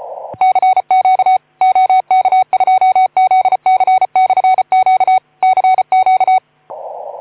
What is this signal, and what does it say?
Morse code, heard as a keyed tone, about 34 wpm, 750 Hz. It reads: CQ OK2ZC/Q CQ
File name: Archiv / CUC181 / OK2ZC CUC 181